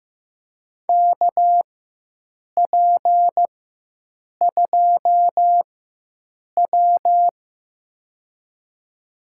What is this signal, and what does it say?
Morse code, heard as a keyed tone, about 15 wpm, spaced as for 7 wpm, 700 Hz. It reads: KP2W
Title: USA Call Signs - 4x Character Spacing - 15wpm